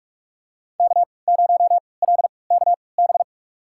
Morse code, heard as a keyed tone, about 46 wpm, spaced as for 26 wpm, 700 Hz. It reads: K0LKB
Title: USA Call Signs - 5 Characters - 3x Character Spacing - 45wpm